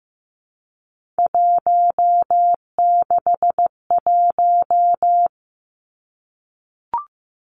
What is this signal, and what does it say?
Morse code, heard as a keyed tone, about 15 wpm, 700 Hz. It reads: 161
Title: Three Challenging Letter-Number Combinations 15wpm